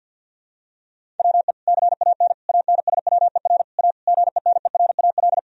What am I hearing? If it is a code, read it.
Morse code, heard as a keyed tone, about 50 wpm, 700 Hz. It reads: WE CAN ANSWER A GENERAL
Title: Sentences from Top 1000 Words - Review Part 3 - 50wpm